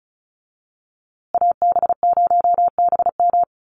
Morse code, heard as a keyed tone, about 34 wpm, 700 Hz. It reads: AB0BM
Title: USA Call Signs Repeated 2x 35wpm